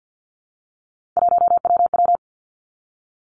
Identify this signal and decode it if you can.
Morse code, heard as a keyed tone, about 50 wpm, 700 Hz. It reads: 1WW
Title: Three Challenging Letter-Number Combinations 50wpm